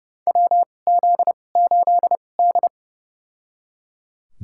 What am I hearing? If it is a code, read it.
Morse code, heard as a keyed tone, about 31 wpm, spaced as for 22 wpm, 700 Hz. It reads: WZ8D
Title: USA Call Signs - 4 Characters - 2x Character Spacing - 30wpm